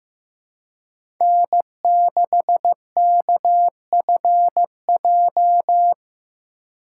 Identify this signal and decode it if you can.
Morse code, heard as a keyed tone, about 15 wpm, 700 Hz. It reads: N6KFJ
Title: USA Call Signs - 3x Character Spacing - 15wpm